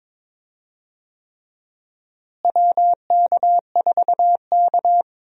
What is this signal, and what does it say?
Morse code, heard as a keyed tone, about 22 wpm, 700 Hz. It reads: WK4K
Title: USA Call Signs - 4 Characters - Repeated 3x - 22wpm